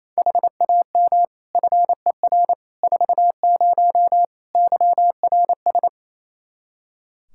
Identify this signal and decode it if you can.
Morse code, heard as a keyed tone, about 28 wpm, 700 Hz. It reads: HAM FER 40 YRS
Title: Practice QSO - Exchange 3 - More Info - 28wpm